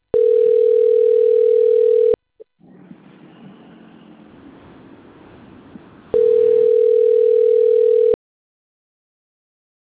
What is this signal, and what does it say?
Ringback tone (North America), 440 + 480 Hz.